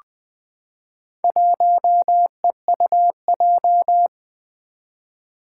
Morse code, heard as a keyed tone, about 20 wpm, 700 Hz. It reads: 1EUJ
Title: Four Challenging Letter-Number Combinations 20wpm